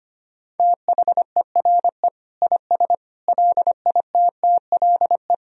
Morse code, heard as a keyed tone, about 25 wpm, 700 Hz. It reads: THERE IS LITTLE
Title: Sentences from Top 1000 Words - Review Part 4 - 25wpm